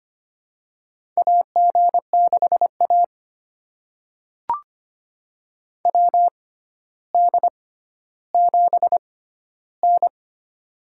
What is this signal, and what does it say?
Morse code, heard as a keyed tone, about 25 wpm, spaced as for 13 wpm, 700 Hz. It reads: AG6A WD7N
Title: USA Call Signs - 6x Character Spacing - 25wpm